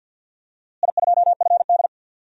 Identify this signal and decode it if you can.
Morse code, heard as a keyed tone, about 48 wpm, 700 Hz. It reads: I2FD
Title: DXCC-100 Call Signs - 4 Characters - Repeated 3x - 50wpm